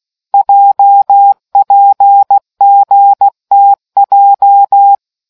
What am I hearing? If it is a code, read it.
Morse code, heard as a keyed tone, about 16 wpm, 800 Hz. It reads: JPGTJ